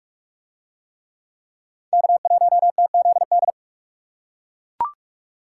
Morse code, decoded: K1TZD

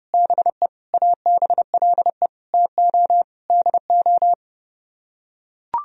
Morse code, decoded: BE ABLE TO DO

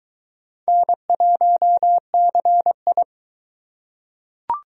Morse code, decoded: N1CI

